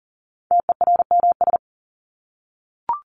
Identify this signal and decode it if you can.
Morse code, heard as a keyed tone, about 40 wpm, 700 Hz. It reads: TERMS
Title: ES:S Suffix - Top 100 Words with Base Words 40wpm